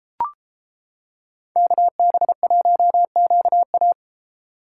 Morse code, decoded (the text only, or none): KB1QA